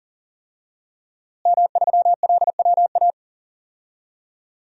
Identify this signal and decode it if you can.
Morse code, heard as a keyed tone, about 40 wpm, 700 Hz. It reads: M3LWA
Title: DXCC-100 Call Signs - 5 Characters - Repeated 3x - 40wpm